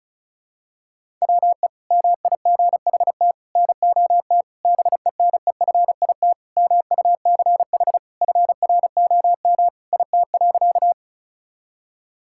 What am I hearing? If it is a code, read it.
Morse code, heard as a keyed tone, about 34 wpm, 700 Hz. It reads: WE MIGHT NOT BENEFIT MUCH FROM IT.
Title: Sentences from Top 1000 Words - Review Part 6 - 35wpm